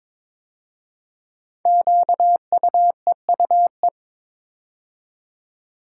Morse code, heard as a keyed tone, about 22 wpm, 700 Hz. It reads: QUEUE
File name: 5 Letter Words - First 3 Letters Given - 22wpm